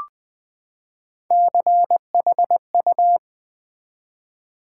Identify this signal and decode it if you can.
Morse code, heard as a keyed tone, about 20 wpm, 700 Hz. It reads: CHU
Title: Build A Word - 3 Letters At A Time - 6-Letter Words - 20wpm